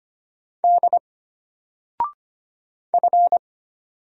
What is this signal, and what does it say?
Morse code, heard as a keyed tone, about 25 wpm, 700 Hz. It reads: D F
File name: Single-Letter-Number-25wpm